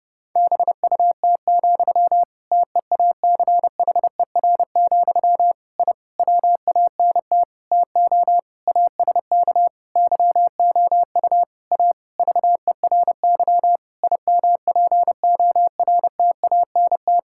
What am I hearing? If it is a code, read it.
Morse code, heard as a keyed tone, about 32 wpm, 700 Hz. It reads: BUT, TEACHER, I WANT TO ASK YOU A VERY IMPORTANT
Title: Sentences from Top 1000 Words - Review Part 6 - 30wpm